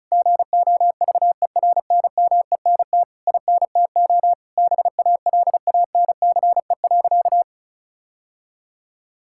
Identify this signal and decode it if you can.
Morse code, heard as a keyed tone, about 34 wpm, 700 Hz. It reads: GOVERNMENT INTO BALANCE.
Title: Sentences from Top 1000 Words - Review Part 6 - 35wpm